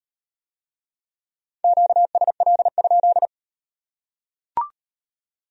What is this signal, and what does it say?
Morse code, heard as a keyed tone, about 38 wpm, 700 Hz. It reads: QSL?